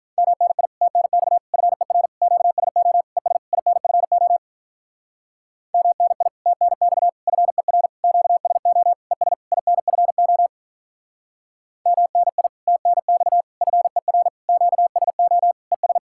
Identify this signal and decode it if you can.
Morse code, heard as a keyed tone, about 43 wpm, 700 Hz. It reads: MNI TNX FER QSO ES INFO MNI TNX FER QSO ES INFO MNI TNX FER QSO ES
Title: Speed-Racing Common QSO Phrases Encore - 35wpm